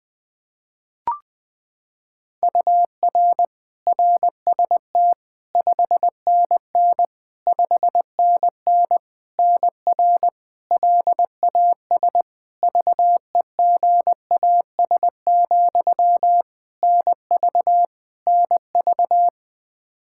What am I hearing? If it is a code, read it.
Morse code, heard as a keyed tone, about 20 wpm, 700 Hz. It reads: UR RST 5NN 5NN NR LAS VEGAS, NV NV
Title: K1USN SST QSO Practice 20wpm